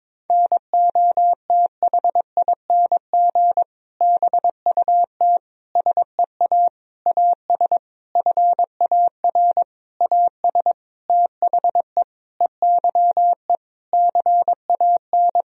Morse code, heard as a keyed tone, about 22 wpm, 700 Hz. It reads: NOTHING BUT SEA AS FAR AS THE EYE CAN
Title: Sentences from Top 1000 Words - Review Part 1 - 22wpm